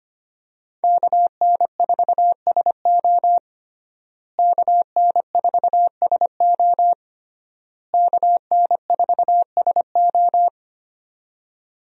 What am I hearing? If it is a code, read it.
Morse code, heard as a keyed tone, about 25 wpm, 700 Hz. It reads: KN4SO KN4SO KN4SO
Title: USA Call Signs - 5 Characters - Repeated 3x - 25wpm